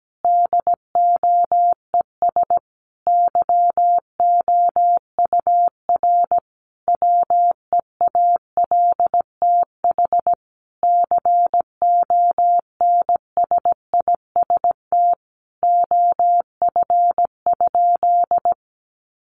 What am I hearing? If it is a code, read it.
Morse code, heard as a keyed tone, about 17 wpm, 700 Hz. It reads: DOES YOUR WEALTH CONSIST OF?
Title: Sentences from Top 1400 Words 17wpm